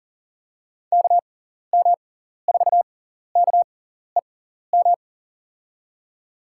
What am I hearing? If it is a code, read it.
Morse code, heard as a keyed tone, about 40 wpm, spaced as for 14 wpm, 700 Hz. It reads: KM4KEM
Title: USA Call Signs - 6 Characters - 6x Character Spacing - 40wpm